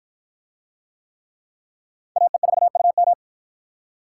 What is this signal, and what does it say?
Morse code, heard as a keyed tone, about 51 wpm, 700 Hz. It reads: AE4UM